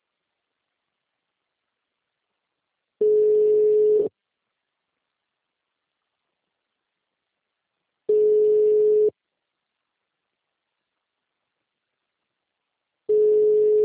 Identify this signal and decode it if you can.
Ringback tone (Europe, 425 Hz).
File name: out- -2028- - - .55